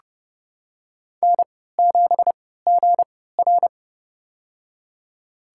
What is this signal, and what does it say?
Morse code, heard as a keyed tone, about 30 wpm, spaced as for 17 wpm, 700 Hz. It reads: N7GR
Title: USA Call Signs - 3x Character Spacing - 30wpm